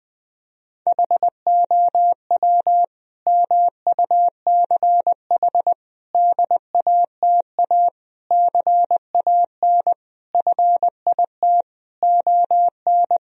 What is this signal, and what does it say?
Morse code, heard as a keyed tone, about 20 wpm, 700 Hz. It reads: HOW MUCH DATA CAN FIT ON
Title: Sentences from Top 1000 Words - Review Part 4 - 20wpm